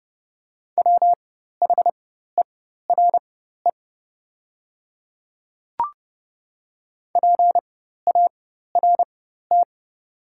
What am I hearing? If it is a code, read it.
Morse code, heard as a keyed tone, about 30 wpm, spaced as for 14 wpm, 700 Hz. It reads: WHERE PART
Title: 4x Character Spacing - Top 200 Words - 30wpm